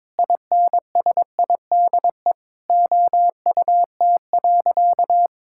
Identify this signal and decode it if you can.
Morse code, heard as a keyed tone, about 22 wpm, 700 Hz. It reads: INSIDE OUT.
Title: Sentences from Top 1600 Words 22wpm